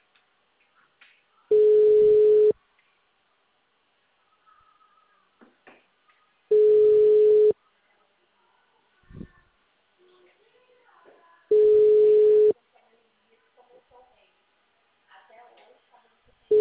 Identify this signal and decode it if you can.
Ringback tone (Europe, 425 Hz), with speech.
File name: out- -2028- - - .55